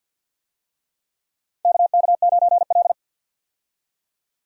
Morse code, heard as a keyed tone, about 54 wpm, 700 Hz. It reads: KK9L
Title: USA Call Signs - 4 Characters - 50wpm